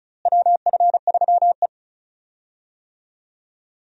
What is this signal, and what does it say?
Morse code, heard as a keyed tone, about 36 wpm, 700 Hz. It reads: WF3E
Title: USA Call Signs - 4 Characters - Repeated 2x - 35wpm